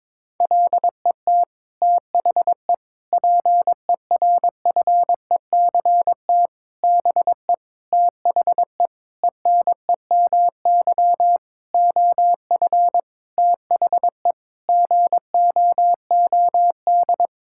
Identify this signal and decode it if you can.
Morse code, heard as a keyed tone, about 22 wpm, 700 Hz. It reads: LET THE PERFECT BE THE ENEMY OF THE GOOD
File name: Sets of 9 Words - 22wpm